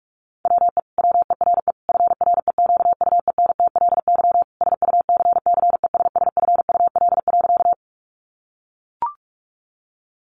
Morse code, heard as a keyed tone, about 44 wpm, 700 Hz. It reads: WE WERE FREQUENTLY SUCCESSFUL.
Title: Sentences from Top 1300 Words 45wpm